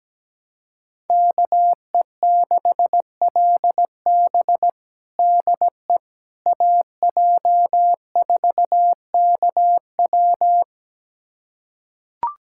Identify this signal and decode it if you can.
Morse code, heard as a keyed tone, about 17 wpm, 700 Hz. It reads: KE6LB DE AJ4KW
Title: USA Callsign DE Callsign - 5 Characters - 17wpm